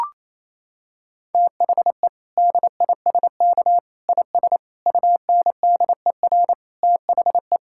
Morse code, heard as a keyed tone, about 28 wpm, 700 Hz. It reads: THE DISK IS UNDER THE
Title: Sentences from Top 1000 Words - Review Part 6 - 28wpm